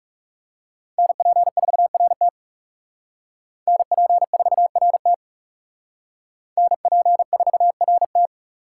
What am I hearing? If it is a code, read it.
Morse code, heard as a keyed tone, about 36 wpm, 700 Hz. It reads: NP4RT NP4RT NP4RT
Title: Speed Racing - USA Call Signs 30wpm